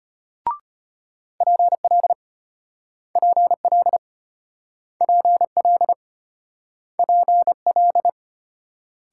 Morse code, heard as a keyed tone, about 35 wpm, 700 Hz. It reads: PL PL PL PL